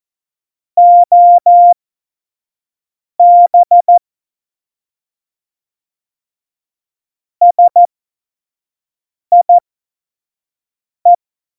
Morse code, heard as a keyed tone, about 14 wpm, spaced as for 5 wpm, 700 Hz. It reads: OB SIE
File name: Morsekurs / Witze / Witz 21 Juli 2020, 25 F70